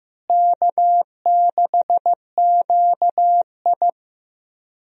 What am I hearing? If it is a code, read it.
Morse code, heard as a keyed tone, about 15 wpm, 700 Hz. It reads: K6QI
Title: USA Call Signs - 4 Characters - Repeated 3x - 15wpm